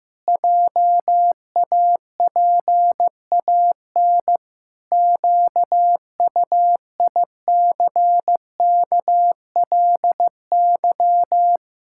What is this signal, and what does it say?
Morse code, heard as a keyed tone, about 16 wpm, 700 Hz. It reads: JAPAN QUICKLY
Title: Pangram Challenge v2 - 15wpm